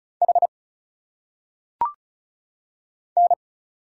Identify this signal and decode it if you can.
Morse code, heard as a keyed tone, about 36 wpm, 700 Hz. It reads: H N
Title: Single Letters - Phonetic Pronunciation 35wpm